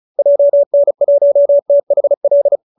Morse code, heard as a keyed tone, about 34 wpm, 590 Hz. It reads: JN1THL